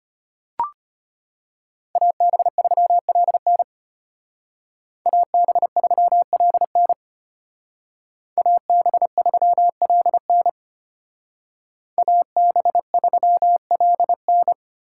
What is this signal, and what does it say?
Morse code, heard as a keyed tone, about 32 wpm, 700 Hz. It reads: AB3LN AB3LN AB3LN AB3LN